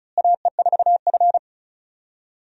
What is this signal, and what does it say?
Morse code, heard as a keyed tone, about 36 wpm, 700 Hz. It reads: AE4F
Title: USA Call Signs - 4 Characters - Repeated 3x - 35wpm